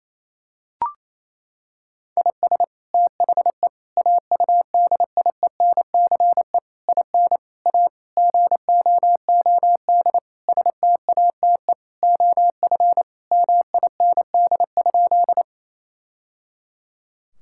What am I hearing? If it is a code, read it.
Morse code, heard as a keyed tone, about 28 wpm, 700 Hz. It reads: IS THE AUDIENCE IN A GOOD STATE OF MIND?
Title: Sentences from Top 1000 Words - Review Part 5 - 28wpm